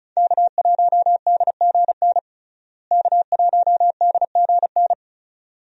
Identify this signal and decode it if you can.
Morse code, heard as a keyed tone, about 34 wpm, 700 Hz. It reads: K1DGN K1DGN
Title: USA Call Signs - 5 Characters - Repeated 2x - 35wpm